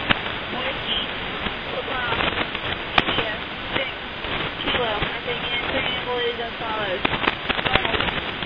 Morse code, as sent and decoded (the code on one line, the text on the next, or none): none